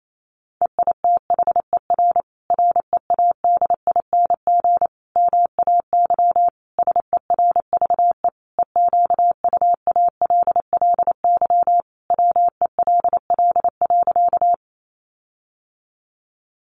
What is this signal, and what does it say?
Morse code, heard as a keyed tone, about 28 wpm, 700 Hz. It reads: EITHER READING MAY SERVE EQUALLY WELL.
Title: Sentences from Top 1400 Words 28wpm